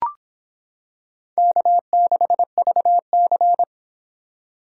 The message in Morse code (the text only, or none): K6VC